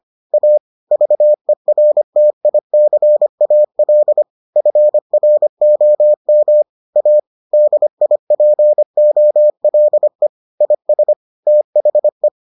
Morse code, heard as a keyed tone, about 25 wpm, 600 Hz. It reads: A VERTICAL FROM A DIPOLE IS THE